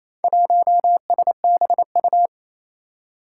Morse code, heard as a keyed tone, about 29 wpm, 700 Hz. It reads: 1SBU
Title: Four Challenging Letter-Number Combinations 28wpm